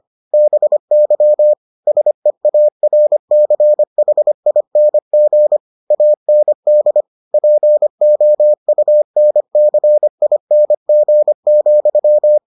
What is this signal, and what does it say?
Morse code, heard as a keyed tone, about 25 wpm, 600 Hz. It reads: BY SEARCHING AND POUNCING,